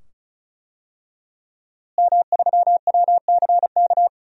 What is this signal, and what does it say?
Morse code, heard as a keyed tone, about 36 wpm, 700 Hz. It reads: M3WCK
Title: DXCC-100 Call Signs - 5 Characters - Repeated 2x - 35wpm